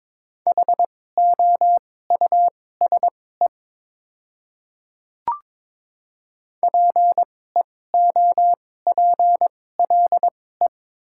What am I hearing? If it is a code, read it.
Morse code, heard as a keyed tone, about 22 wpm, spaced as for 16 wpm, 700 Hz. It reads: HOUSE PEOPLE
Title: 2x Character Spacing - Top 300 Words - 22wpm